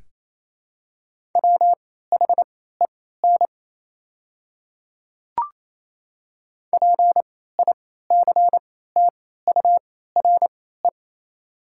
Morse code, heard as a keyed tone, about 28 wpm, spaced as for 16 wpm, 700 Hz. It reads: WHEN PICTURE
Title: 3x Character Spacing - Top 200 Words - 28wpm